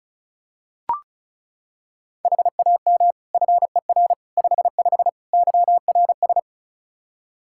Morse code, heard as a keyed tone, about 36 wpm, 700 Hz. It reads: HAM FER 55 YRS
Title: Practice QSO - Exchange 3 - More Info - 35wpm